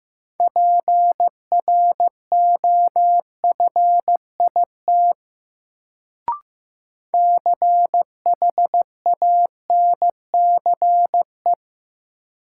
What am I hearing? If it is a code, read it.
Morse code, heard as a keyed tone, about 15 wpm, 700 Hz. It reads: PROFIT CHANCE
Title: Sentences from Top 800 Words - New Word Drill - 15wpm